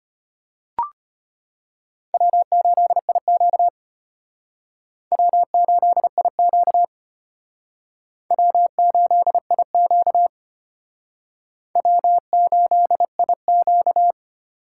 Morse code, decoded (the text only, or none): W8IQ W8IQ W8IQ W8IQ